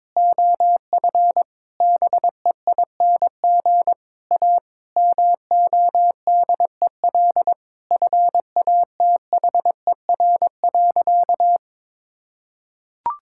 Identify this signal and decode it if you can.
Morse code, heard as a keyed tone, about 22 wpm, 700 Hz. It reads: OF BEING A MODEL FATHER.
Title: Sentences from Top 1000 Words - Review Part 5 - 22wpm